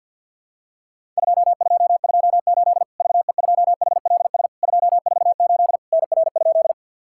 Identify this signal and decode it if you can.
Morse code, heard as a keyed tone, about 50 wpm, 700 Hz. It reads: 2228 VE2HLS 248 NR?
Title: Worked All Europe DX Contest - QTC Exchange with Repeats - 50wpm